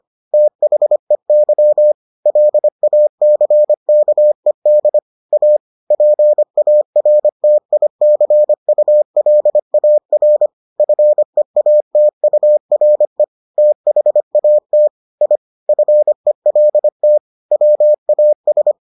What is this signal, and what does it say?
Morse code, heard as a keyed tone, about 25 wpm, 600 Hz. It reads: THEY LACKED A PARTICULAR FEATURE THAT I FELT WAS